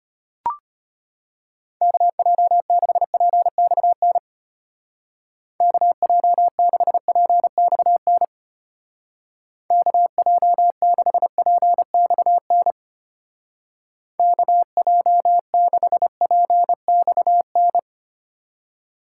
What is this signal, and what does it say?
Morse code, heard as a keyed tone, about 32 wpm, 700 Hz. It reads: KJ6PXN KJ6PXN KJ6PXN KJ6PXN